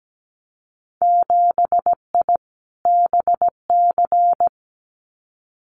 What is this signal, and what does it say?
Morse code, heard as a keyed tone, about 17 wpm, 700 Hz. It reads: 7I BC
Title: Winter Field Day Exchanges Encore 17wpm